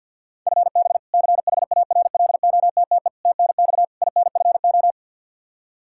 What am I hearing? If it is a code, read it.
Morse code, heard as a keyed tone, about 52 wpm, 700 Hz. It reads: FB CHARLOTTE TNX INFO
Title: Practice QSO - Exchange 3 - More Info - 50wpm